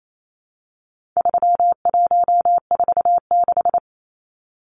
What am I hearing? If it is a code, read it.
Morse code, heard as a keyed tone, about 28 wpm, 700 Hz. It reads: 3146
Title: Numbers - 4-Digit 28wpm